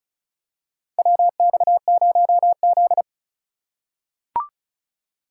Morse code, heard as a keyed tone, about 36 wpm, 700 Hz. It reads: WX0Z
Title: K1USN SST QSO Practice - 35wpm